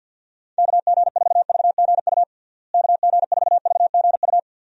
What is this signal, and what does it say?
Morse code, heard as a keyed tone, about 53 wpm, 700 Hz. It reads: KG4VGU KG4VGU
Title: USA Call Signs - 6 Characters - Repeated 2x - 50wpm